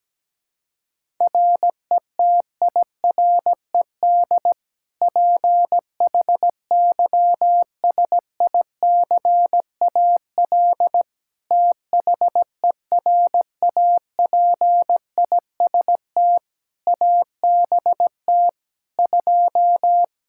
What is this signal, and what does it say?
Morse code, heard as a keyed tone, about 17 wpm, 700 Hz. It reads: RETIRED PHYSICAL THERAPIST ABT 2
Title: K1USN SST QSO Practice 17wpm